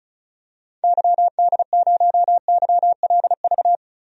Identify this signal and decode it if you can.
Morse code, heard as a keyed tone, about 36 wpm, 700 Hz. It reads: YD0YLV
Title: DXCC-100 Call Signs - 6 Characters - 35wpm